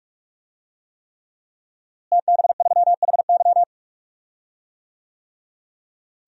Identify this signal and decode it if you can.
Morse code, heard as a keyed tone, about 46 wpm, 700 Hz. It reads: TB3HY
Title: DXCC-100 Call Signs - 5 Characters - Repeated 3x - 45wpm